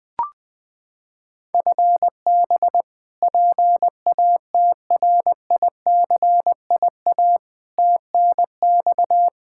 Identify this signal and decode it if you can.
Morse code, heard as a keyed tone, about 21 wpm, 700 Hz. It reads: FB PATRICIA TNX
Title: Practice QSO - Exchange 3 - More Info - 20wpm